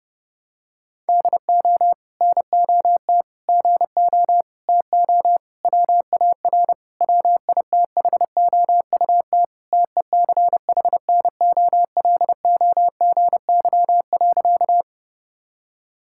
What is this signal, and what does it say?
Morse code, heard as a keyed tone, about 32 wpm, 700 Hz. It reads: DO NOT GO TO WAR WITHOUT TECHNOLOGY.
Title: Sentences from Top 1000 Words - Review Part 5 - 30wpm